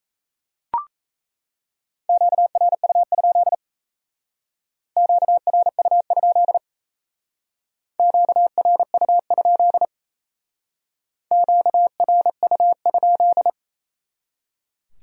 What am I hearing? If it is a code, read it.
Morse code, heard as a keyed tone, about 35 wpm, 700 Hz. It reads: QRU? QRU? QRU? QRU?
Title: Speed-Racing Q-Codes - 28wpm